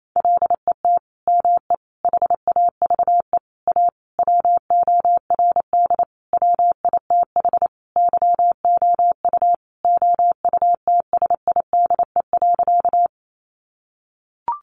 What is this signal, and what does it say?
Morse code, heard as a keyed tone, about 28 wpm, 700 Hz. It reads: LET ME HAVE A WORD WITH YOU OUTSIDE.
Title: Sentences from Top 900 Words 28wpm